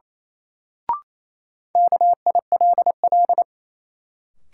Morse code, heard as a keyed tone, about 28 wpm, 700 Hz. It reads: KILL
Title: Sentences from Top 1400 Words - New Word Drill - 28wpm